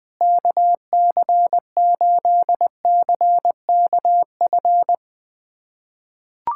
Morse code, heard as a keyed tone, about 21 wpm, 700 Hz. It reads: KC8CKF
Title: USA Callsign DE Callsign - 6 Characters - 20wpm